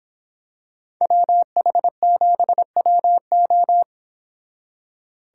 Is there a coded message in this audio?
Yes: Morse code from 1.0 to 3.8 s, heard as a keyed tone, about 26 wpm, 700 Hz. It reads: WH7WO